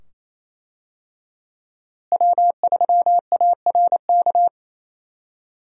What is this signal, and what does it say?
Morse code, heard as a keyed tone, about 28 wpm, 700 Hz. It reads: W3ARK
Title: USA Call Signs - 5 Characters - Repeated 2x - 28wpm